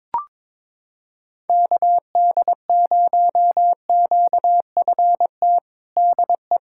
Morse code, heard as a keyed tone, about 22 wpm, 700 Hz. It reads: KD0QFT DE
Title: USA Callsign DE Callsign - 6 Characters - 22wpm